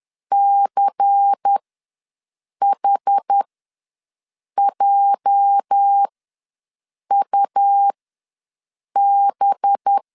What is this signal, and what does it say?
Morse code, heard as a keyed tone, about 10 wpm, spaced as for 6 wpm, 800 Hz. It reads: CHJUB
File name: LEZIONI / 8 YJ45 / 8 YJ ws 10wpm 4043